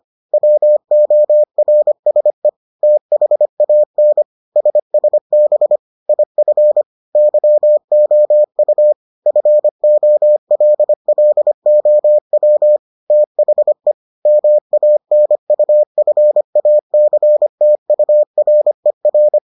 Morse code, heard as a keyed tone, about 25 wpm, 600 Hz. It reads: WORSE THAN SSB IF YOU FOLLOW THE MANUFACTURER